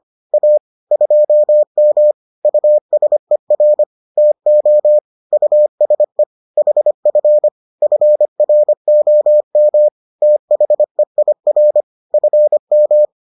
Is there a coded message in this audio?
Yes: Morse code from 0.3 to 13.1 s, heard as a keyed tone, about 25 wpm, 600 Hz. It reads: A 2M USER TO USE HF FROM THEIR FM